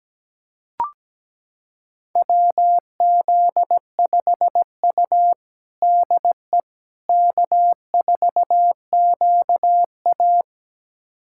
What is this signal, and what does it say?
Morse code, heard as a keyed tone, about 17 wpm, 700 Hz. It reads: WZ5U DE K4QA
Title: USA Callsign DE Callsign - 4 Characters - 17wpm